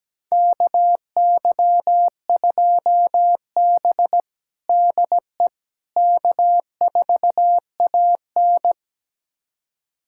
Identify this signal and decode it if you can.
Morse code, heard as a keyed tone, about 17 wpm, 700 Hz. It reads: KY2B DE K4AN
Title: USA Callsign DE Callsign - 4 Characters - 17wpm